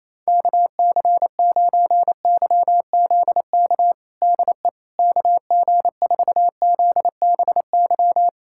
Morse code, heard as a keyed tone, about 28 wpm, 700 Hz. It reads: KC9YZK DE KG4ZBY
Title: USA Callsign DE Callsign 28wpm